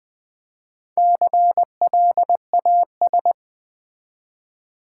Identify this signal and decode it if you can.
Morse code, heard as a keyed tone, about 20 wpm, 700 Hz. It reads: CLAS
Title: Build A Word - 4 Letters At A Time - 8-Letter Words - 20wpm